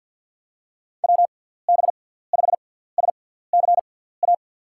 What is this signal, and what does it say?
Morse code, heard as a keyed tone, about 53 wpm, 700 Hz. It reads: W B 5 S C A
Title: USA Call Signs - 6 Characters - 6x Character Spacing - 50wpm